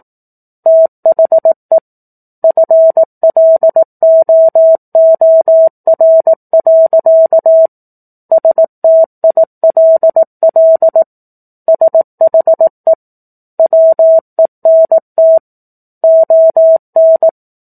Morse code, heard as a keyed tone, about 18 wpm, 650 Hz. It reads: THE FLOOR. STILL SHE WENT ON
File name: 04-Chapter-04 18 WPM